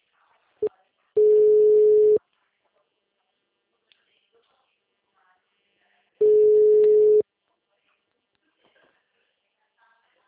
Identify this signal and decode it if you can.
Ringback tone (Europe, 425 Hz).